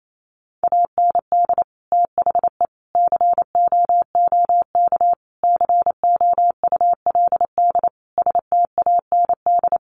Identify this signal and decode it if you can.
Morse code, heard as a keyed tone, about 28 wpm, 700 Hz. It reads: AND THE COOK COULD STAND
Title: Sentences from Top 1300 Words 28wpm